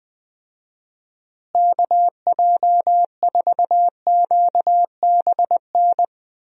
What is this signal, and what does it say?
Morse code, heard as a keyed tone, about 20 wpm, 700 Hz. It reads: KJ4QBN
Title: USA Call Signs - 6 Characters - Repeated 2x - 20wpm